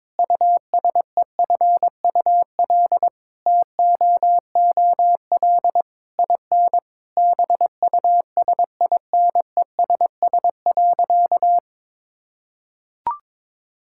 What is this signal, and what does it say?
Morse code, heard as a keyed tone, about 22 wpm, 700 Hz. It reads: USEFUL TOOL IN BUSINESS.
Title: Sentences from Top 1000 Words - Review Part 6 - 22wpm